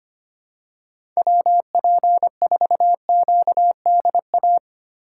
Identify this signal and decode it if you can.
Morse code, heard as a keyed tone, about 25 wpm, 700 Hz. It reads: WP4QDA
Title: USA Call Signs - 8x Character Spacing - 25wpm